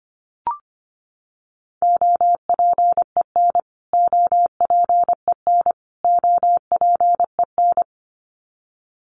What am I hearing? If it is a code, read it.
Morse code, heard as a keyed tone, about 25 wpm, 700 Hz. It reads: OPEN OPEN OPEN
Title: Top 300 Words Repeated 25wpm